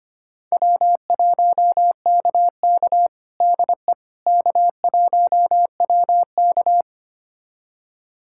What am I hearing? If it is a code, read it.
Morse code, heard as a keyed tone, about 25 wpm, 700 Hz. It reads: W1KK DE K1WK
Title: USA Callsign DE Callsign - 4 Characters - 25wpm